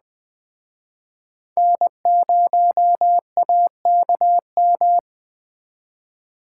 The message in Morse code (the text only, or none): N0AKM